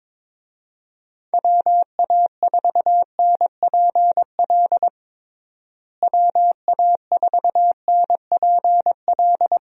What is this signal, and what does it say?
Morse code, heard as a keyed tone, about 21 wpm, 700 Hz. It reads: WA4NPL WA4NPL
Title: USA Call Signs - 6 Characters - Repeated 3x - 22wpm